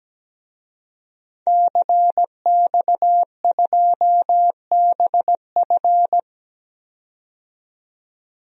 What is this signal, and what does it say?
Morse code, heard as a keyed tone, about 17 wpm, 700 Hz. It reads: CX2BF
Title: DXCC-100 Call Signs - 5 Characters - 17wpm